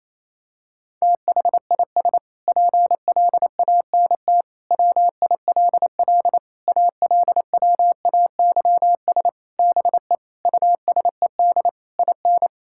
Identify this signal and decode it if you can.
Morse code, heard as a keyed tone, about 28 wpm, 700 Hz. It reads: THIS PLANT WILL ALWAYS BE USED IN
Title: Sentences from Top 1000 Words - Review Part 6 - 28wpm